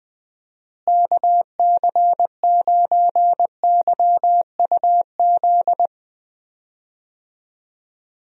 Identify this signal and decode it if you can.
Morse code, heard as a keyed tone, about 20 wpm, 700 Hz. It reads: KC9YUZ
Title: USA Call Signs - 6 Characters - Repeated 2x - 20wpm